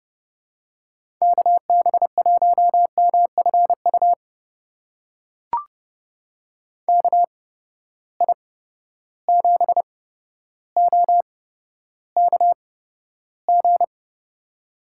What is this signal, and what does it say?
Morse code, heard as a keyed tone, about 30 wpm, spaced as for 13 wpm, 700 Hz. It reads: KB1MFU KI7OKG
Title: USA Call Signs - 6 Characters - 8x Character Spacing - 30wpm